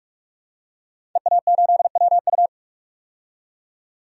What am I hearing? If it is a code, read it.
Morse code, heard as a keyed tone, about 46 wpm, 700 Hz. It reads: EA8WU
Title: DXCC-100 Call Signs - 5 Characters - 45wpm